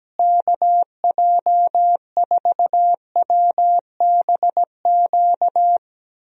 Morse code, heard as a keyed tone, about 17 wpm, 700 Hz. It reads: KJ4WBQ
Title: USA Call Signs - 6 Characters - Repeated 2x - 17wpm